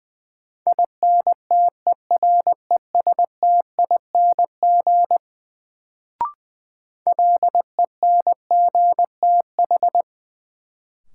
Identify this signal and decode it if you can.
Morse code, heard as a keyed tone, about 20 wpm, 700 Hz. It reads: INTERESTING LENGTH
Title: Sentences from Top 1000 Words - New Word Drill - 20wpm